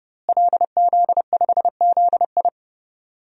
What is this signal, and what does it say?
Morse code, heard as a keyed tone, about 30 wpm, 700 Hz. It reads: LZ5ZI
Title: DXCC-100 Call Signs - 5 Characters - Repeated 2x - 30wpm